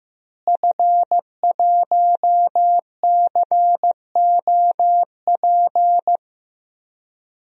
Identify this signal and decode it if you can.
Morse code, heard as a keyed tone, about 15 wpm, 700 Hz. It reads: F1COP
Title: DXCC-100 Call Signs - 5 Characters - Repeated 3x - 15wpm